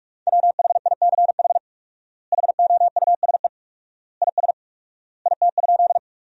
Morse code, heard as a keyed tone, about 42 wpm, 700 Hz. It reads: WHICH HOUSE IS IT?
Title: 4x Word-Spacing - Sentences from Top 200 Words - 45wpm